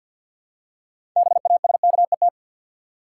Morse code, heard as a keyed tone, about 52 wpm, 700 Hz. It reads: BASKET
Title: Six Letter Words - 50wpm